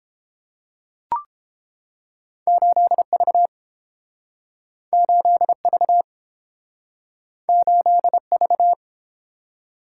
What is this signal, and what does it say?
Morse code, heard as a keyed tone, about 30 wpm, 700 Hz. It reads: 8V 8V 8V